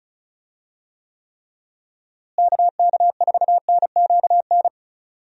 Morse code, heard as a keyed tone, about 34 wpm, 700 Hz. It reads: KK4NQN
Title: USA Call Signs - 6 Characters - Repeated 3x - 35wpm